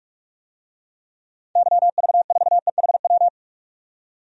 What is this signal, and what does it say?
Morse code, heard as a keyed tone, about 41 wpm, 700 Hz. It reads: YV4EHW
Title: DXCC-100 Call Signs - 6 Characters - 45wpm